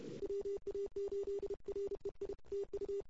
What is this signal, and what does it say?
Morse code, heard as a keyed tone, about 32 wpm, 410 Hz. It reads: WA8REI TU
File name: CW / 2008-ss-cw-k5tr-op-n5rz -22:00